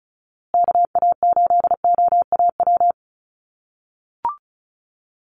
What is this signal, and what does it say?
Morse code, heard as a keyed tone, about 34 wpm, 700 Hz. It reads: KA8OAW
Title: USA Call Signs Repeated 3x 35wpm